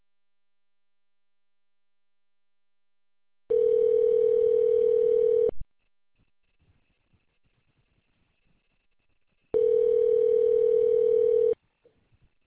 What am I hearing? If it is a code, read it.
Ringback tone (North America), 440 + 480 Hz.